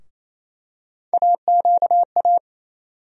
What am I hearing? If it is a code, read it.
Morse code, heard as a keyed tone, about 28 wpm, 700 Hz. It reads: AQA
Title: USA Call Sign Postfixes - 3 Letters 28wpm